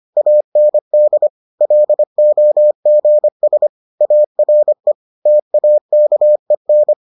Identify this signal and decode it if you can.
Morse code, heard as a keyed tone, about 25 wpm, 600 Hz. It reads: AND LOGS ARE TAKEN